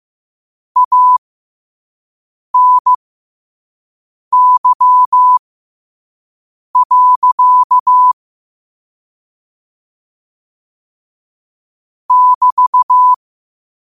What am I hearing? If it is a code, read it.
Morse code, heard as a keyed tone, about 15 wpm, spaced as for 5 wpm, 1000 Hz. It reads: ANY. <BT>
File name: qotdfiles / 5wpm